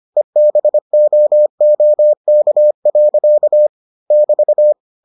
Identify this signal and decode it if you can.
Morse code, heard as a keyed tone, about 25 wpm, 600 Hz. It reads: EBOOK. <BT>